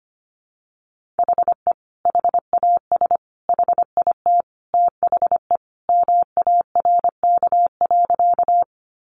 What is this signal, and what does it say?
Morse code, heard as a keyed tone, about 25 wpm, 700 Hz. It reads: HE HAS HIT THE MARK.
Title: Sentences from Top 1200 Words 25wpm